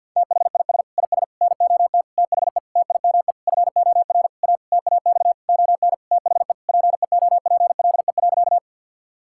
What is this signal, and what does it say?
Morse code, heard as a keyed tone, about 46 wpm, 700 Hz. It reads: THIS IS NOT THE TIME FOR A TAX ON THE PEOPLE.
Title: Sentences from Top 1000 Words - Review Part 5 - 50wpm